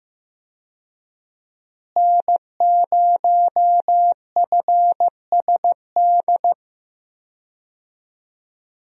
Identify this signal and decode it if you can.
Morse code, heard as a keyed tone, about 16 wpm, 700 Hz. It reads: N0FSD
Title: USA Call Signs - 5 Characters - Repeated 3x - 15wpm